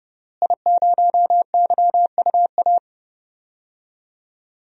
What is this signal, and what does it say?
Morse code, heard as a keyed tone, about 30 wpm, 700 Hz. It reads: I0YUA
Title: DXCC-100 Call Signs - 5 Characters - Repeated 3x - 30wpm